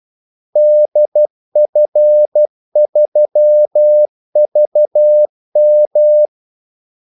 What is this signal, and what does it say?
Morse code, heard as a keyed tone, about 12 wpm, 600 Hz. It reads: DF3VM
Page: 2 ekstra ordmellemrum mellem kaldesignalerne